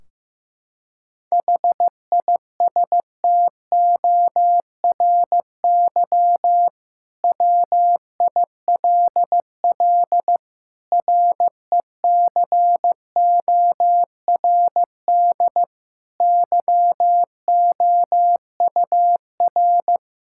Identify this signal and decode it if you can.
Morse code, heard as a keyed tone, about 15 wpm, 700 Hz. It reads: HISTORY WILL RECORD YOUR
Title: Sentences from Top 1600 Words 15wpm